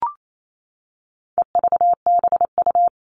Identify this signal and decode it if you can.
Morse code, heard as a keyed tone, about 28 wpm, 700 Hz. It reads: EVBU
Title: All Cut Numbers - 4-Digit 28wpm